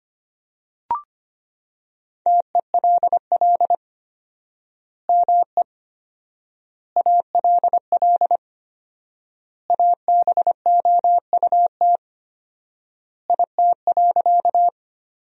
Morse code, heard as a keyed tone, about 25 wpm, 700 Hz. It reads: TELL ME ALL ABOUT IT.
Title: 4x Word-Spacing - Sentences from Top 200 Words - 25wpm